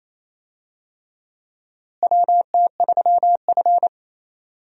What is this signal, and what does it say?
Morse code, heard as a keyed tone, about 28 wpm, 700 Hz. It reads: WT3F
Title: USA Call Signs - 4 Characters - Repeated 3x - 28wpm